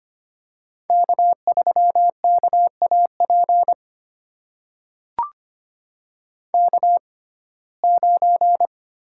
Morse code, heard as a keyed tone, about 25 wpm, spaced as for 18 wpm, 700 Hz. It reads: K3KAP K9
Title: USA Call Signs - 5 Characters - 6x Character Spacing - 25wpm